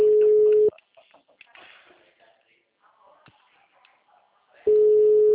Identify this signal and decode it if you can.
Ringback tone (Europe, 425 Hz), with speech.
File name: out- -2017- - - .71243